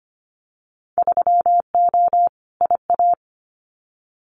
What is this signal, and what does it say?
Morse code, heard as a keyed tone, about 25 wpm, 700 Hz. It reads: 3O IA
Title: Winter Field Day Exchanges 25wpm